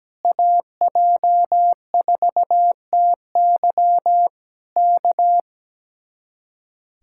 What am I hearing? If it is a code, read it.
Morse code, heard as a keyed tone, about 17 wpm, 700 Hz. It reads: AJ4TY K
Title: Speed-Racing Common QSO Phrases - 17wpm